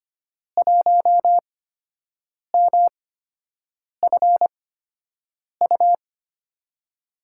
Morse code, heard as a keyed tone, about 25 wpm, spaced as for 7 wpm, 700 Hz. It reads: 1MFU
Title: USA Call Signs - 6 Characters - 8x Character Spacing - 25wpm